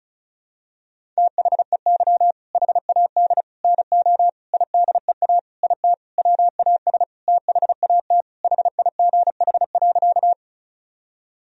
Morse code, heard as a keyed tone, about 34 wpm, 700 Hz. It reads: THEY HAD NO IDEA IT WAS THAT HIGH.
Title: Sentences from Top 1000 Words - Review Part 1 - 35wpm